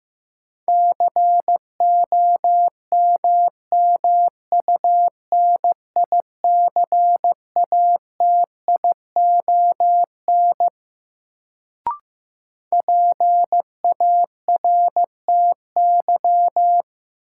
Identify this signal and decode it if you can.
Morse code, heard as a keyed tone, about 15 wpm, 700 Hz. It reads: COMMUNICATION PARTY
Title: Sentences from Top 1400 Words - New Word Drill Alt - 15wpm